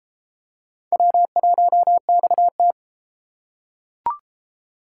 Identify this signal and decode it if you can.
Morse code, heard as a keyed tone, about 32 wpm, 700 Hz. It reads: W1XT